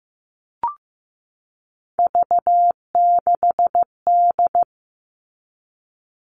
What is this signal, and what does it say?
Morse code, heard as a keyed tone, about 15 wpm, 700 Hz. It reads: V6D
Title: All Cut Numbers - 3-Digit 15wpm